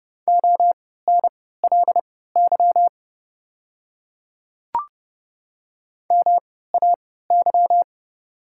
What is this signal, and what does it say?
Morse code, heard as a keyed tone, about 30 wpm, spaced as for 17 wpm, 690 Hz. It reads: ONLY MAY
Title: 3x Character Spacing - Top 200 Words - 30wpm